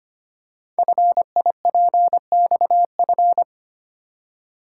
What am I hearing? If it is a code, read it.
Morse code, heard as a keyed tone, about 25 wpm, 700 Hz. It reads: FIPXF
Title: Code Groups - 5 Letters - 25wpm